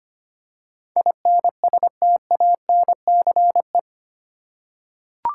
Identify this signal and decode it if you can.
Morse code, heard as a keyed tone, about 25 wpm, 700 Hz. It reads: INSTANCE
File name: Eight Letter Words - 25wpm